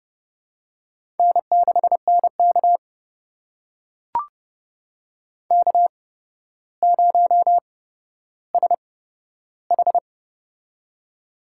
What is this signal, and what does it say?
Morse code, heard as a keyed tone, about 30 wpm, spaced as for 13 wpm, 700 Hz. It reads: N6NK K0SH
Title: USA Call Signs - 4 Characters - 8x Character Spacing - 30wpm